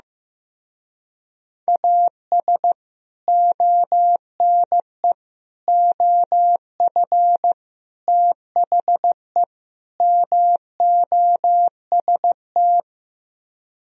Morse code, heard as a keyed tone, about 15 wpm, 700 Hz. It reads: AS ONE OF THE MOST
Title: Speed-Racing - Sets of 5 Words 15wpm